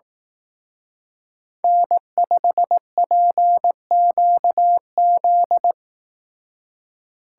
Morse code, heard as a keyed tone, about 18 wpm, 700 Hz. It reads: N5PQZ
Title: Speed Racing - USA 1x3 Call Signs 15wpm